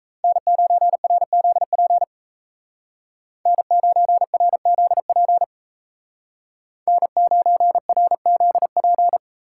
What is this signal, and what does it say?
Morse code, heard as a keyed tone, about 34 wpm, 700 Hz. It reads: N9RZP N9RZP N9RZP
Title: Speed Racing - USA 1x3 Call Signs 28wpm